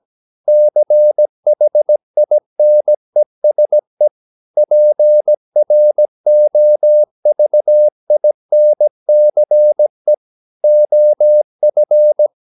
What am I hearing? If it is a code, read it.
Morse code, heard as a keyed tone, about 17 wpm, 600 Hz. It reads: CHINESE PROVINCE OF
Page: Romanen Never af Ken Follet på engelsk.